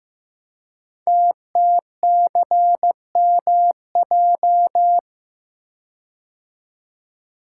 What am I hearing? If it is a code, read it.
Morse code, heard as a keyed tone, about 15 wpm, 700 Hz. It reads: TTCMJ
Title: Code Groups - 5 Letters - 15wpm